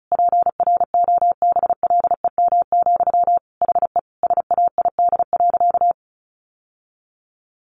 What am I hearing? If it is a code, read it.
Morse code, heard as a keyed tone, about 34 wpm, 700 Hz. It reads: PROBLEM, HE SAID.
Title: Sentences from Top 1000 Words 35wpm